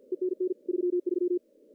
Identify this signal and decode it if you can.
Morse code, heard as a keyed tone, about 48 wpm, 370 Hz. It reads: ENN 33
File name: CW / cqww-cw-k5tr